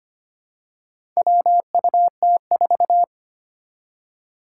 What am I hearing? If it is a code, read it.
Morse code, heard as a keyed tone, about 25 wpm, 700 Hz. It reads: WUT4
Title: Four Challenging Letter-Number Combinations 25wpm